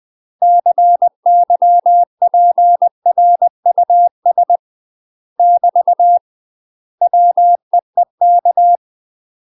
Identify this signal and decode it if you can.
Morse code, heard as a keyed tone, about 20 wpm, 700 Hz. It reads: CYPRUS <BT> WEEK